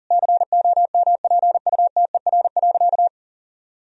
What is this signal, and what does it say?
Morse code, heard as a keyed tone, about 40 wpm, 700 Hz. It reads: COMPUTER.
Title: Sentences from Top 1000 Words - Review Part 3 - 40wpm